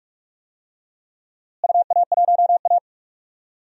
Morse code, heard as a keyed tone, about 42 wpm, 700 Hz. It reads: UA1A
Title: DXCC-100 Call Signs - 4 Characters - Repeated 2x - 45wpm